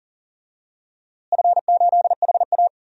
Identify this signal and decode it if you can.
Morse code, heard as a keyed tone, about 38 wpm, 700 Hz. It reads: F8HA
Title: DXCC-100 Call Signs - 4 Characters - Repeated 2x - 40wpm